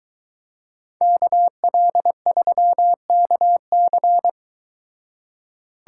Morse code, heard as a keyed tone, about 22 wpm, 700 Hz. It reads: KL3KC